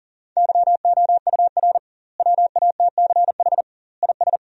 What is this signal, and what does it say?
Morse code, heard as a keyed tone, about 40 wpm, 700 Hz. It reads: YOUR WATCH IS
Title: 2x Word-Spacing - Sentences from Top 400 Words - 40wpm